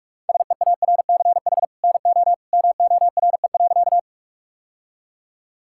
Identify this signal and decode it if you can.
Morse code, heard as a keyed tone, about 42 wpm, 700 Hz. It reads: SEARCH NO MORE.
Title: Sentences from Top 1000 Words - Review Part 5 - 45wpm